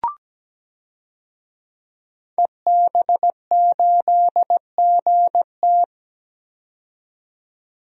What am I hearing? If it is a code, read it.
Morse code, heard as a keyed tone, about 17 wpm, 700 Hz. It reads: EB8GT
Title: DXCC-100 Call Signs - 5 Characters - 17wpm